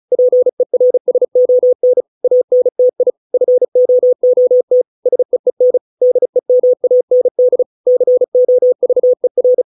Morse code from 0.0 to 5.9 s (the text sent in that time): PERSON ANTI FOOT SEEN